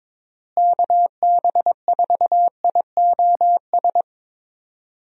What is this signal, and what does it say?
Morse code, heard as a keyed tone, about 22 wpm, 700 Hz. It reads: KB4IOS
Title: USA Call Signs - 6 Characters - Repeated 2x - 22wpm